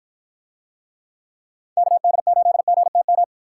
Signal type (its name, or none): Morse code